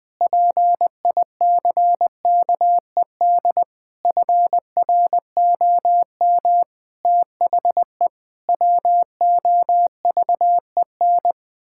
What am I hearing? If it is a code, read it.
Morse code, heard as a keyed tone, about 21 wpm, 700 Hz. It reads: PICKED FROM THE WOVEN
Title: Pangram Challenge v2 - 20wpm